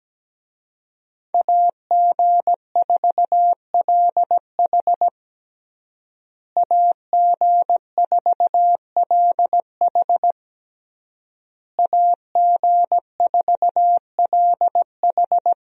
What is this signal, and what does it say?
Morse code, heard as a keyed tone, about 17 wpm, 700 Hz. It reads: AG4LH AG4LH AG4LH
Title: USA Call Signs - 5 Characters - Repeated 3x - 17wpm